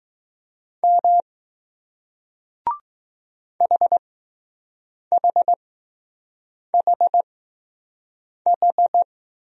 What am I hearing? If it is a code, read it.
Morse code, heard as a keyed tone, about 20 wpm, spaced as for 6 wpm, 700 Hz. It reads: M HHHH